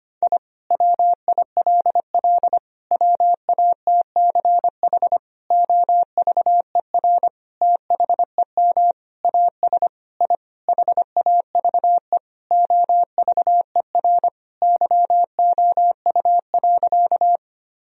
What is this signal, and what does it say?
Morse code, heard as a keyed tone, about 25 wpm, 700 Hz. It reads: I WILL WATCH OVER THEM AS I HAVE OVER YOU.
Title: Sentences from Top 1000 Words - Review Part 1 - 25wpm